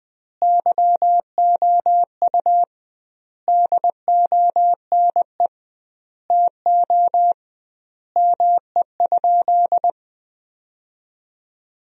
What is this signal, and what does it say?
Morse code, heard as a keyed tone, about 20 wpm, 700 Hz. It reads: YOU DONE TO ME?
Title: 2x Word-Spacing - Sentences from Top 500 Words - 20wpm